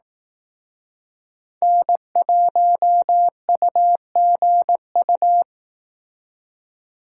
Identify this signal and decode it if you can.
Morse code, heard as a keyed tone, about 19 wpm, 700 Hz. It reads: N1UGU
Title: Speed Racing - USA 1x3 Call Signs 15wpm